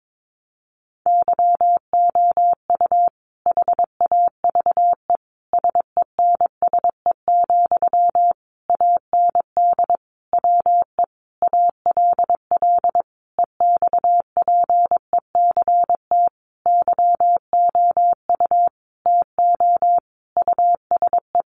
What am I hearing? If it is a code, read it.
Morse code, heard as a keyed tone, about 22 wpm, 700 Hz. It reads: YOU HAVE SENSE, AND WE ALL EXPECT YOU TO USE
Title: Sentences from Top 1100 Words 22wpm